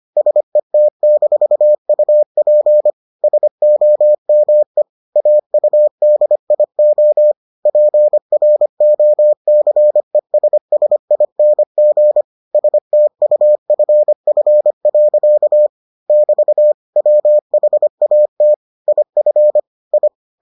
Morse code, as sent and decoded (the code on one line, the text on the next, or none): ... - -....- ..- .--. / ... --- -- . / .- ..- -.. .. --- / .--. .-. --- -.-. ... ... .. -. --. / ... - ..- ..-. ..-. .-.-.- / -...- / .-- .... .- - / .. ..-. / ..
SET-UP SOME AUDIO PROCESSING STUFF. <BT> WHAT IF I